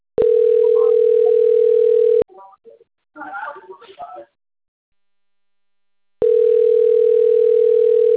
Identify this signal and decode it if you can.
Ringback tone (North America), 440 + 480 Hz, with speech.
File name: rg-1600-unknown- - - .6503